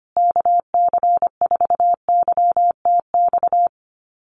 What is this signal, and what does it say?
Morse code, heard as a keyed tone, about 25 wpm, 700 Hz. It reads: KC4YTX
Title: USA Call Signs Repeated 3x 25wpm